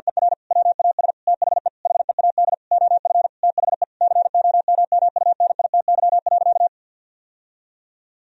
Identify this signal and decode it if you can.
Morse code, heard as a keyed tone, about 50 wpm, 700 Hz. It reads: ER WAS THE HEAD OF THE COMMUNITY.
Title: Sentences from Top 1000 Words - Review Part 4 - 50wpm